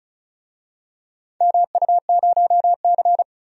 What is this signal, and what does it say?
Morse code, heard as a keyed tone, about 36 wpm, 700 Hz. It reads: MU0C
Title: DXCC-100 Call Signs - 4 Characters - 35wpm